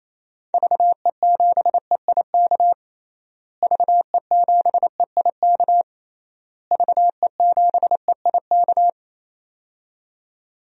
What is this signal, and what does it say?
Morse code, heard as a keyed tone, about 28 wpm, 700 Hz. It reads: VE7EIK VE7EIK VE7EIK
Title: DXCC-100 Call Signs - 6 Characters - Repeated 3x - 28wpm